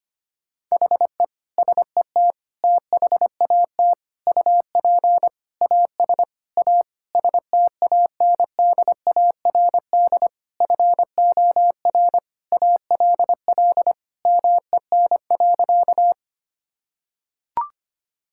Morse code, heard as a keyed tone, about 25 wpm, 700 Hz. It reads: HE SET THAT UP AS A STANDARD FOR ALL MEN.
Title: Sentences from Top 1000 Words - Review Part 6 - 25wpm